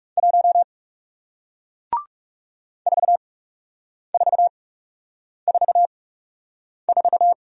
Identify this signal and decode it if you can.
Morse code, heard as a keyed tone, about 39 wpm, spaced as for 8 wpm, 700 Hz. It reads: 1 4444